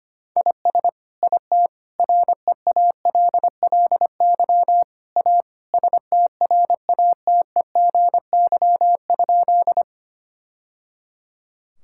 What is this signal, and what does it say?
Morse code, heard as a keyed tone, about 25 wpm, 700 Hz. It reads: IS IT REALLY A STRATEGY?
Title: Sentences from Top 1000 Words - Review Part 5 - 25wpm